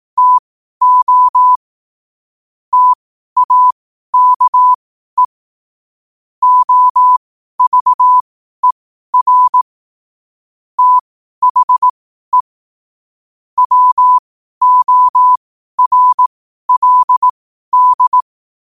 Morse code, heard as a keyed tone, about 18 wpm, spaced as for 12 wpm, 1000 Hz. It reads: TO TAKE OVER THE WORLD
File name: qotdfiles / 12wpm